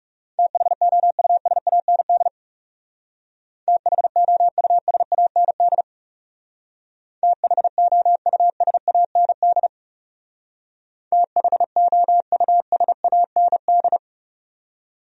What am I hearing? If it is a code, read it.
Morse code, heard as a keyed tone, about 37 wpm, 700 Hz. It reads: THOUSAND THOUSAND THOUSAND THOUSAND
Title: Speed-Racing Top 500 Words 30wpm